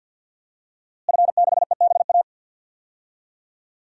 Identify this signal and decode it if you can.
Morse code, heard as a keyed tone, about 47 wpm, 700 Hz. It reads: F6EBA
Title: DXCC-100 Call Signs - 5 Characters - Repeated 3x - 50wpm